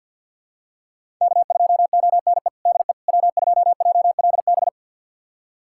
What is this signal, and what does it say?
Morse code, heard as a keyed tone, about 50 wpm, 700 Hz. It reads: K2ONE DE W2JLB
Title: USA Callsign DE Callsign - 5 Characters - 50wpm